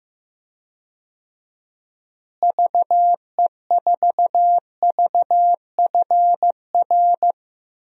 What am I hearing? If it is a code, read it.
Morse code, heard as a keyed tone, about 15 wpm, 700 Hz. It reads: VE4VFR
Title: DXCC-100 Call Signs - 6 Characters - Repeated 2x - 15wpm